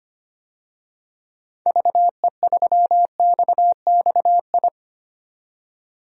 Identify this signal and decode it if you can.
Morse code, heard as a keyed tone, about 25 wpm, 700 Hz. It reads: VE3XXI
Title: DXCC-100 Call Signs - 6 Characters - 25wpm